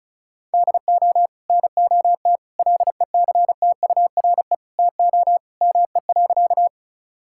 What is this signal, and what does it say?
Morse code, heard as a keyed tone, about 36 wpm, 700 Hz. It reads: DO NOT LECTURE TO ME.
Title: Sentences from Top 1600 Words 35wpm